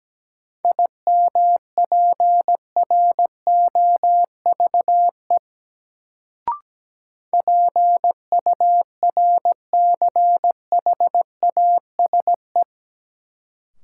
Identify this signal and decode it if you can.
Morse code, heard as a keyed tone, about 17 wpm, 700 Hz. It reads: IMPROVE PURCHASE
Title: Sentences from Top 800 Words - New Word Drill - 17wpm